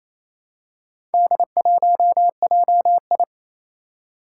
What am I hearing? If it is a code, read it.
Morse code, heard as a keyed tone, about 29 wpm, 700 Hz. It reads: D1JI
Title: Four Challenging Letter-Number Combinations 28wpm